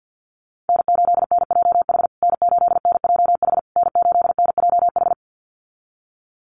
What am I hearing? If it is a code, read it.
Morse code, heard as a keyed tone, about 50 wpm, 700 Hz. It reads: N8NJH N8NJH N8NJH
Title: USA Call Signs Repeated 3x 50wpm